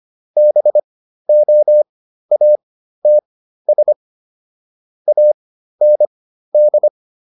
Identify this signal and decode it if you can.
Morse code, heard as a keyed tone, about 25 wpm, spaced as for 13 wpm, 600 Hz. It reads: BOATS AND